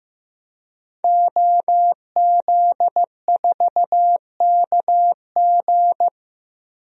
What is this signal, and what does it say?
Morse code, heard as a keyed tone, about 15 wpm, 700 Hz. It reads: OZ4KG
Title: DXCC-100 Call Signs - 5 Characters - Repeated 3x - 15wpm